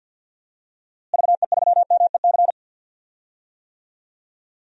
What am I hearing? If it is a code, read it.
Morse code, heard as a keyed tone, about 52 wpm, 700 Hz. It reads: VE3MEC
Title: DXCC-100 Call Signs - 6 Characters - Repeated 3x - 50wpm